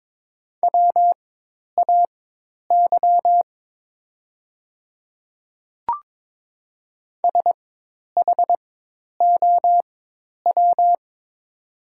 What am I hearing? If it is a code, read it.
Morse code, heard as a keyed tone, about 22 wpm, spaced as for 10 wpm, 700 Hz. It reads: WAY SHOW
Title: 4x Character Spacing - Top 200 Words - 22wpm